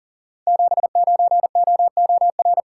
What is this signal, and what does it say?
Morse code, heard as a keyed tone, about 40 wpm, 700 Hz. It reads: 79OOR
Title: Code Groups - 5 Letter:Numbers - 40wpm